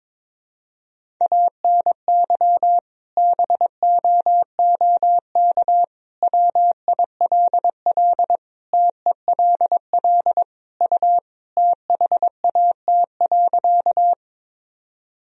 Morse code, heard as a keyed tone, about 22 wpm, 700 Hz. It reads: ANY BOOK WILL TELL U THAT.
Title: Sentences from Top 300 Words and Abbreviations 22wpm